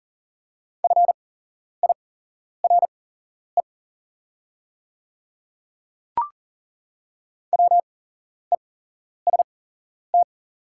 Morse code, heard as a keyed tone, about 40 wpm, spaced as for 11 wpm, 690 Hz. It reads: FIRE WEST
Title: 8x Character Spacing - Top 500 Words - 40wpm